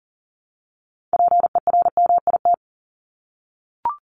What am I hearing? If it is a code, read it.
Morse code, heard as a keyed tone, about 40 wpm, 700 Hz. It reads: PERMIT
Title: Top 1751-2000 Words 40wpm